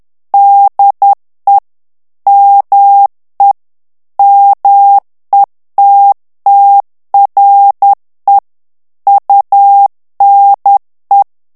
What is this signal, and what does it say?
Morse code, heard as a keyed tone, about 10 wpm, 800 Hz. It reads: DE ME METTRE UNE
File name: morse mp3 / FRANCAIS TEXTE CLAIR 2 VITESSE 10 MOTS MINUTES